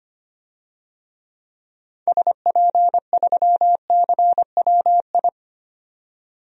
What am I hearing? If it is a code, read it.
Morse code, heard as a keyed tone, about 25 wpm, 700 Hz. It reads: SP3CWI
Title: DXCC-100 Call Signs - 6 Characters - 25wpm